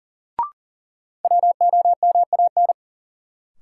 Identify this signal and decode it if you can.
Morse code, heard as a keyed tone, about 40 wpm, 700 Hz. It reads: WOMAN
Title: Sets of 5 Words - New Word Drill - 40wpm